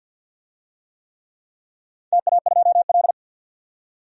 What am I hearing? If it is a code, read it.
Morse code, heard as a keyed tone, about 52 wpm, 700 Hz. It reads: TA2L
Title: DXCC-100 Call Signs - 4 Characters - 50wpm